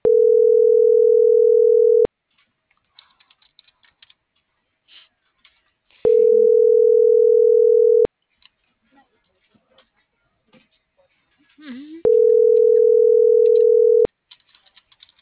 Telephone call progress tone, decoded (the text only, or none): ringback tone